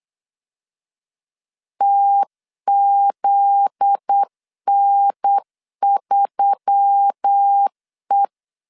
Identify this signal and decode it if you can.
Morse code, heard as a keyed tone, about 8 wpm, 800 Hz. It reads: TZN3E
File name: LEZIONI / 7 GF9Z / 7 GF9Z 8wpm 3213